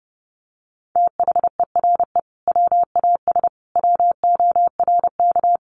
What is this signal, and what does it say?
Morse code, heard as a keyed tone, about 28 wpm, 700 Hz. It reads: THERE WAS WORK
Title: Sentences from Top 200 Words 30wpm